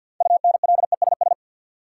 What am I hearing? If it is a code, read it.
Morse code, heard as a keyed tone, about 50 wpm, 700 Hz. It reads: UNLESS
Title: Sentences from Top 1300 Words - New Word Drill Alt - 50wpm